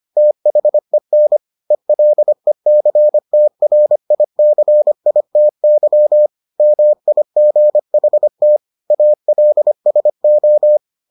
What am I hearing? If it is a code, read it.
Morse code, heard as a keyed tone, about 25 wpm, 600 Hz. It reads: THEN ELECTRICITY MIGHT ALSO